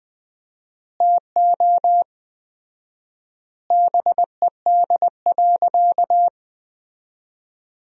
Morse code, heard as a keyed tone, about 20 wpm, 700 Hz. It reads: TO BED.
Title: 4x Word-Spacing - Sentences from Top 500 Words - 20wpm